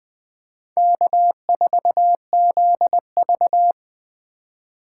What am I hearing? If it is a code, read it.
Morse code, heard as a keyed tone, about 24 wpm, 700 Hz. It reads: K4ZV
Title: Speed Racing - USA 1x2 Call Signs 20wpm